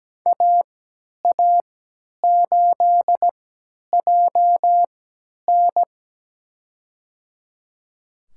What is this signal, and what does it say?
Morse code, heard as a keyed tone, about 17 wpm, spaced as for 10 wpm, 700 Hz. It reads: AA8JN
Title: USA Call Signs - 5 Characters - 3x Character Spacing - 17wpm